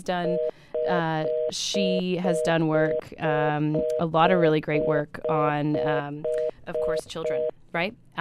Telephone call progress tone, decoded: reorder tone